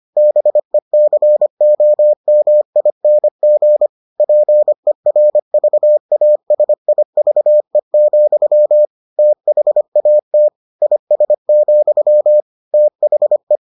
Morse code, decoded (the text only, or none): BECOMING PERVASIVE, THAT IS, THE